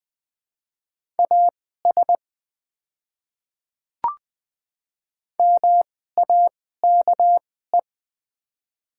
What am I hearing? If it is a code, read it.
Morse code, heard as a keyed tone, about 20 wpm, spaced as for 14 wpm, 700 Hz. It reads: AS MAKE
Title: 2x Character Spacing - Top 300 Words - 20wpm